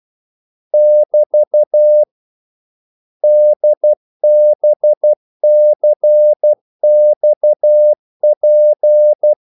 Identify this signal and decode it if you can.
Morse code, heard as a keyed tone, about 12 wpm, 600 Hz. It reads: <BT> DBCXP
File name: code-exemple